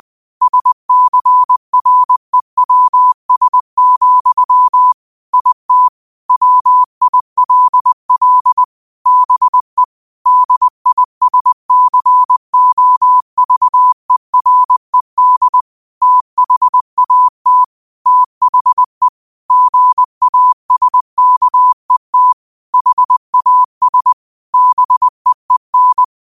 Morse code, heard as a keyed tone, about 20 wpm, 1000 Hz. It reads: SCREWS, IT WILL BE DISCOVERED THAT THE GASKET HAS BEEN